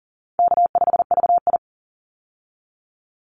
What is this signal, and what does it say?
Morse code, heard as a keyed tone, about 38 wpm, 700 Hz. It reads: K5VI
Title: Easy Call Signs Repeated 2x - 40wpm